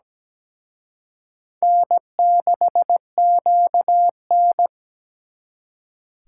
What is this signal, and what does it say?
Morse code, heard as a keyed tone, about 20 wpm, 700 Hz. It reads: N6QN
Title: Speed Racing - USA 1x2 Call Signs 17wpm